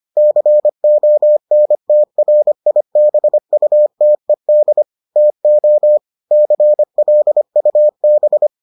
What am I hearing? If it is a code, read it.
Morse code, heard as a keyed tone, about 25 wpm, 600 Hz. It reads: CONTRIBUTED TO CLUB